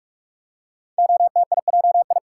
Morse code, heard as a keyed tone, about 44 wpm, 700 Hz. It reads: OTIJI
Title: Code Groups - 5 Letters - 45wpm